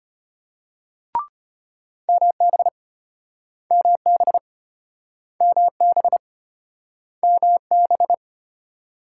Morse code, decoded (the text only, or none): MB MB MB MB